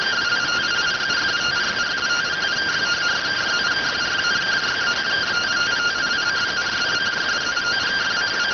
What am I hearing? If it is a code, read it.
Two-tone teleprinter signal, about 170 Hz shift.